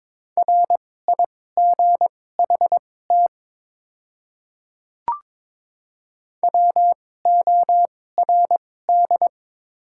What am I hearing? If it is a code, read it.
Morse code, heard as a keyed tone, about 22 wpm, spaced as for 16 wpm, 700 Hz. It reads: RIGHT WORD
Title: 2x Character Spacing - Top 200 Words - 22wpm